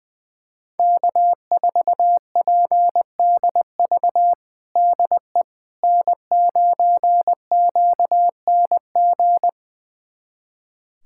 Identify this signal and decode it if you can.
Morse code, heard as a keyed tone, about 21 wpm, 700 Hz. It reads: K4PDV DE N9QNG
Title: USA Callsign DE Callsign 20wpm